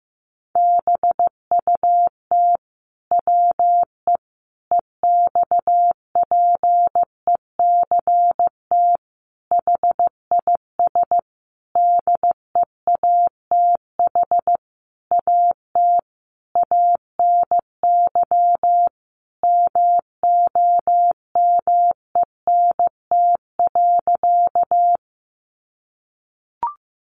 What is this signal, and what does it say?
Morse code, heard as a keyed tone, about 15 wpm, 700 Hz. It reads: BUT WE EXPECT HIS DEATH AT ANY MOMENT.
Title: Sentences from Top 1100 Words 15wpm